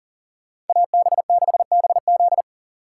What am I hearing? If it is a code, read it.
Morse code, heard as a keyed tone, about 53 wpm, 700 Hz. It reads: AB6BZ